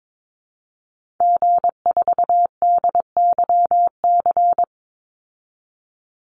Morse code, heard as a keyed tone, about 22 wpm, 700 Hz. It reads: G4DYC
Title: Call Signs Repeated 3x - 22wpm